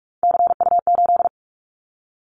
Morse code, heard as a keyed tone, about 46 wpm, 700 Hz. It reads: CU8
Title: Callsign Prefixes for Top 50 DXCC - 45wpm